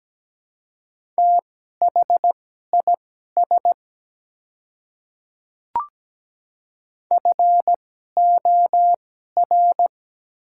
Morse code, heard as a keyed tone, about 17 wpm, spaced as for 12 wpm, 700 Hz. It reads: THIS FOR
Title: 2x Character Spacing - Top 200 Words - 17wpm